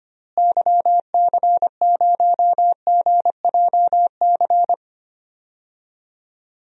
Morse code, heard as a keyed tone, about 25 wpm, 700 Hz. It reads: YC0GJC
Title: DXCC-100 Call Signs - 6 Characters - Repeated 3x - 25wpm